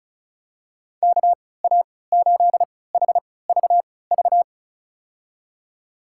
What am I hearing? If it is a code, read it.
Morse code, heard as a keyed tone, about 36 wpm, spaced as for 20 wpm, 700 Hz. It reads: KA8HVV
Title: USA Call Signs - 6 Characters - 3x Character Spacing - 35wpm